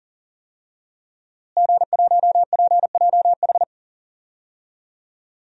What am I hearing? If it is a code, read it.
Morse code, heard as a keyed tone, about 40 wpm, 700 Hz. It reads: G1PJH
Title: DXCC-100 Call Signs - 5 Characters - Repeated 3x - 40wpm